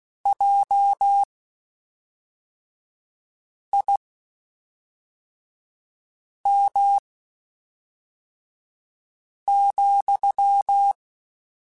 Morse code, heard as a keyed tone, about 16 wpm, spaced as for 3 wpm, 800 Hz. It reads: JIM,
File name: (spaced) / QSOs Spaced-mp3s / WD0BGP 1 @5 sp